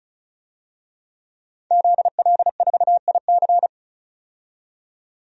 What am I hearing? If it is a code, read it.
Morse code, heard as a keyed tone, about 36 wpm, 700 Hz. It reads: ZL4IC
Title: DXCC-100 Call Signs - 5 Characters - 35wpm